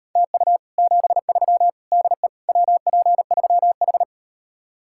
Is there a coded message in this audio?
Yes: Morse code from 0.2 to 4.0 s, heard as a keyed tone, about 38 wpm, 700 Hz. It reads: TU 73 DE WP3H